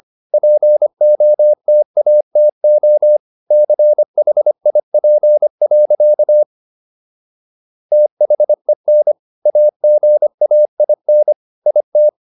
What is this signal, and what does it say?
Morse code, heard as a keyed tone, about 25 wpm, 600 Hz. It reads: POTATO CHIP. THEN AGAIN IT